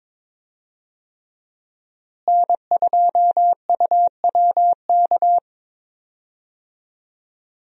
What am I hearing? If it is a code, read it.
Morse code, heard as a keyed tone, about 22 wpm, 700 Hz. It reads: N2UWK
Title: USA Call Signs - 5 Characters - Repeated 3x - 22wpm